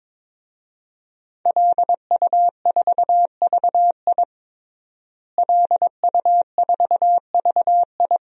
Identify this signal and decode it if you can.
Morse code, heard as a keyed tone, about 22 wpm, 700 Hz. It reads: LU4VI LU4VI
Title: DXCC-100 Call Signs - 5 Characters - Repeated 2x - 22wpm